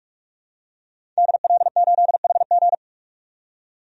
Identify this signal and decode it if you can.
Morse code, heard as a keyed tone, about 46 wpm, 700 Hz. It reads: DL8HG
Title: DXCC-100 Call Signs - 5 Characters - 45wpm